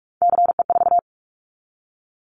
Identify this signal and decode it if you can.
Morse code, heard as a keyed tone, about 44 wpm, 700 Hz. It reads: CE4
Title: Callsign Prefixes for Top 100 DXCC - 45wpm